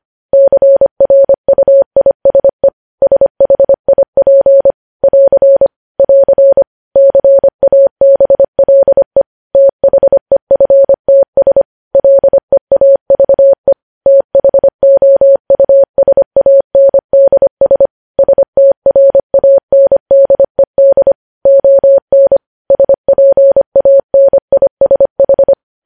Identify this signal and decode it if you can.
Morse code, heard as a keyed tone, about 25 wpm, 550 Hz. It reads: CRUISE SHIP <AR> <AR> CABLE THEFTS LEAVE THOUSANDS STRANDED ON SPANISH
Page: News Headlines in Morse Code at 25 WPM